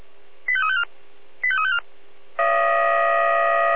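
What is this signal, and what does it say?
Five-tone selective call (ZVEI), heard as quick runs of five tones, tones of about 70 ms. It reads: 85354 85354
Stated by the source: ZVEI five-tone call received on SDR.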